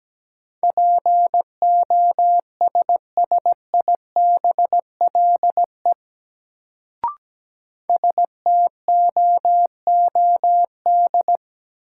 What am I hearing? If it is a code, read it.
Morse code, heard as a keyed tone, about 17 wpm, 700 Hz. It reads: POSSIBLE STOOD
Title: Sentences from Top 500 Words - New Word Drill - 17wpm